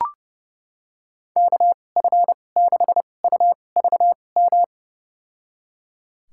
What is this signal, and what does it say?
Morse code, heard as a keyed tone, about 30 wpm, spaced as for 22 wpm, 700 Hz. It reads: KF6UVM
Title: USA Call Signs - 2x Character Spacing - 30wpm